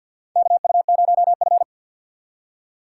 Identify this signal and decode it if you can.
Morse code, heard as a keyed tone, about 50 wpm, 700 Hz. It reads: KU0F
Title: USA Call Signs - 4x Character Spacing - 50wpm